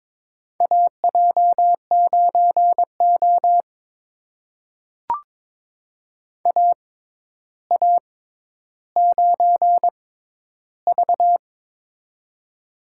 Morse code, heard as a keyed tone, about 22 wpm, spaced as for 11 wpm, 700 Hz. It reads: AJ9O AA9V
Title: USA Call Signs - 4 Characters - 6x Character Spacing - 22wpm